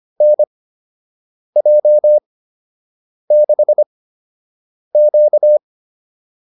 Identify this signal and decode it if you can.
Morse code, heard as a keyed tone, about 25 wpm, spaced as for 7 wpm, 600 Hz. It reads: NJ6Q
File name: QSO 205 07